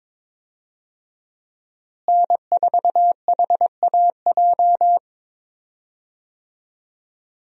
Morse code, heard as a keyed tone, about 22 wpm, 700 Hz. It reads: N4HAJ
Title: USA Call Signs - 5 Characters - Repeated 3x - 22wpm